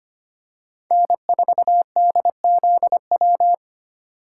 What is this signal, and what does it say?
Morse code, heard as a keyed tone, about 25 wpm, 700 Hz. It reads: N4DZW
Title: USA Call Signs - 5 Characters - 4x Character Spacing - 25wpm